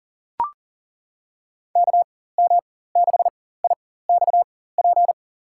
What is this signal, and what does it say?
Morse code, heard as a keyed tone, about 37 wpm, spaced as for 18 wpm, 700 Hz. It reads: KM6IXP
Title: USA Call Signs - 4x Character Spacing - 40wpm